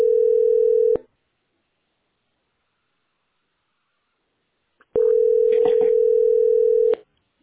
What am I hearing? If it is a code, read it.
Ringback tone (North America), 440 + 480 Hz.